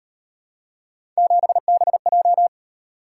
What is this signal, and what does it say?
Morse code, heard as a keyed tone, about 38 wpm, 700 Hz. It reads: 7BJ